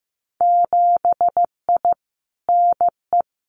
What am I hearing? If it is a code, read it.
Morse code, heard as a keyed tone, about 15 wpm, 700 Hz. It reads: 7I NE
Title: Winter Field Day Exchanges Encore 15wpm